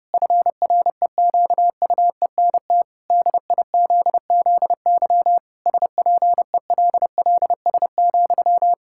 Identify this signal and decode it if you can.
Morse code, heard as a keyed tone, about 30 wpm, 700 Hz. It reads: FREQUENT DIZZY SPELLS,
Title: Pangram Challenge v2 - 30wpm